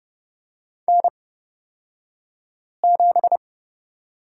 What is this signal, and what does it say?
Morse code, heard as a keyed tone, about 29 wpm, spaced as for 5 wpm, 700 Hz. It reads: N7
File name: Lesson 36d - BT - Two Characters 30wpm